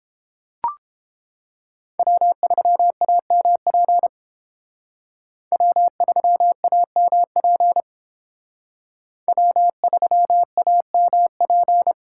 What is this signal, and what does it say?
Morse code, heard as a keyed tone, about 28 wpm, 700 Hz. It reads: W3AMP W3AMP W3AMP